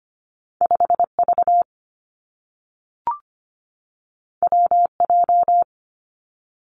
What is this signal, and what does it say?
Morse code, heard as a keyed tone, about 25 wpm, 700 Hz. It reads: 5 V W J
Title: Two Challenging Letter-Number Combinations 25wpm